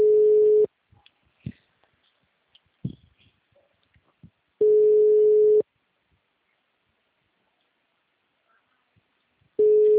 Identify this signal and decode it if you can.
Ringback tone (Europe, 425 Hz).